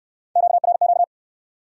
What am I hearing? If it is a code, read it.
Morse code, heard as a keyed tone, about 65 wpm, 700 Hz. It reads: QRQ